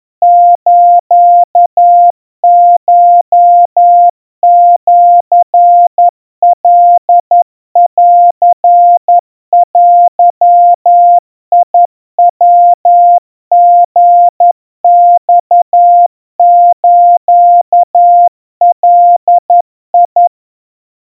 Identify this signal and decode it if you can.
Morse code (Japanese Wabun code), heard as a keyed tone, about 11 wpm, 700 Hz. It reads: スコシカンデヤリマスガ
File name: kawa / Morse sample / kamyu 40 cpm 700 Hz Crispy